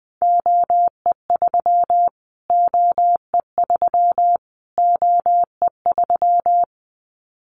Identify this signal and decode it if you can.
Morse code, heard as a keyed tone, about 20 wpm, 700 Hz. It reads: OE3 OE3 OE3
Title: Callsign Prefixes for Top 25 DXCC - Repeated 3x - 20wpm